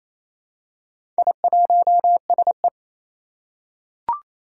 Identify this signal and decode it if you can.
Morse code, heard as a keyed tone, about 29 wpm, 700 Hz. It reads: I1SE
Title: Four Challenging Letter-Number Combinations 28wpm